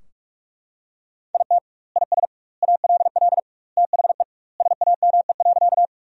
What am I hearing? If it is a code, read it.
Morse code, heard as a keyed tone, about 42 wpm, 700 Hz. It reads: IT IS ALL THE SAME.
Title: 2x Word-Spacing - Sentences from Top 200 Words - 45wpm